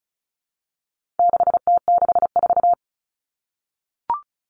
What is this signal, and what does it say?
Morse code, heard as a keyed tone, about 34 wpm, 700 Hz. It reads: 6T64
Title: All Cut Numbers - 4-Digit 35wpm